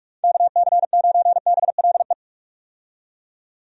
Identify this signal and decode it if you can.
Morse code, heard as a keyed tone, about 42 wpm, 700 Hz. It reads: KC9BLE
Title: USA Call Signs - 6 Characters - 45wpm